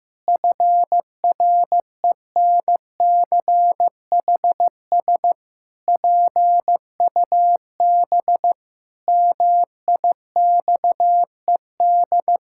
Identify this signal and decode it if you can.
Morse code, heard as a keyed tone, about 15 wpm, 700 Hz. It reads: FRENCHS PUB MIXED
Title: Pangram Challenge v2 - 15wpm